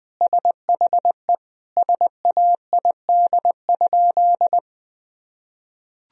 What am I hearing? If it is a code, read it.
Morse code, heard as a keyed tone, about 20 wpm, 700 Hz. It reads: SHE SAID?
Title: Sentences from Top 1000 Words - Review Part 1 - 20wpm